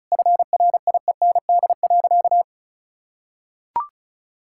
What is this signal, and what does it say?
Morse code, heard as a keyed tone, about 36 wpm, 700 Hz. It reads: FRIEND.
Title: Sentences from Top 1000 Words - Review Part 6 - 35wpm